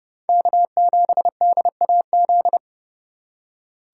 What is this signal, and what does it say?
Morse code, heard as a keyed tone, about 32 wpm, 700 Hz. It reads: K7DAZ
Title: USA Call Signs - 5 Characters - 4x Character Spacing - 30wpm